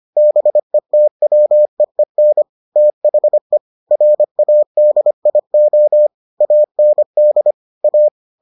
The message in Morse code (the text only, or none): BETWEEN THE RADIO AND A